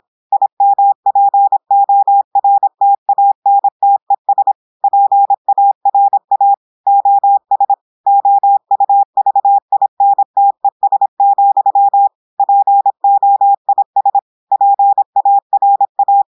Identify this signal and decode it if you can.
Morse code, heard as a keyed tone, about 26 wpm, 800 Hz. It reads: IMPORTANTES PARA OS OUVINTES, POIS PARA
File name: portugais texte en clair 2 vitesse 26 mots minutes